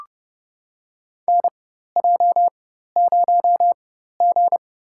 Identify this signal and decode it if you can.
Morse code, heard as a keyed tone, about 31 wpm, spaced as for 14 wpm, 700 Hz. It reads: NJ0G
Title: USA Call Signs - 4 Characters - 4x Character Spacing - 30wpm